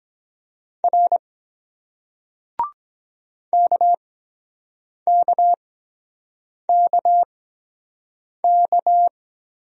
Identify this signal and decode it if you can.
Morse code, heard as a keyed tone, about 23 wpm, spaced as for 7 wpm, 700 Hz. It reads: R KKKK